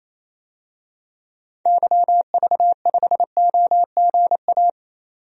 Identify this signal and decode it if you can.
Morse code, heard as a keyed tone, about 28 wpm, 700 Hz. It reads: YV5OGA
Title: DXCC-100 Call Signs - 6 Characters - 28wpm